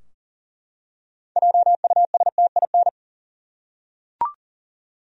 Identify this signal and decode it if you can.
Morse code, heard as a keyed tone, about 40 wpm, 700 Hz. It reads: JUSTIN
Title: USA Common Names - 40wpm